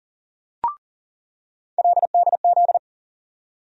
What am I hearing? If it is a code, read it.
Morse code, heard as a keyed tone, about 40 wpm, 700 Hz. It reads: LDZ
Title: USA Call Sign Postfixes - 3 Letters 40wpm